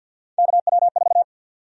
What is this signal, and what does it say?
Morse code, heard as a keyed tone, about 52 wpm, 700 Hz. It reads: KW4
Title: USA Call Sign Prefixes - Repeated 2x - 50wpm